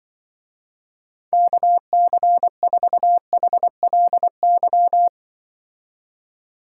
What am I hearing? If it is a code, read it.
Morse code, heard as a keyed tone, about 24 wpm, 700 Hz. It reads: KC4HLY
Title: Speed Racing - USA 2x3 Call Signs 20wpm